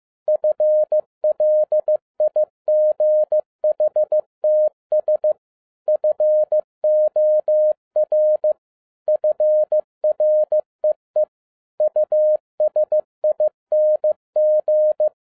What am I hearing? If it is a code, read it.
Morse code, heard as a keyed tone, about 15 wpm, 610 Hz. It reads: FLIGHTS FOR FREE USING
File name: 2011-07-mp3 / n12 T8XO 15wpm